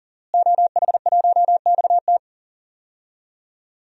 Morse code, heard as a keyed tone, about 37 wpm, 700 Hz. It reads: OH1XT
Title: DXCC-100 Call Signs - 5 Characters - Repeated 3x - 40wpm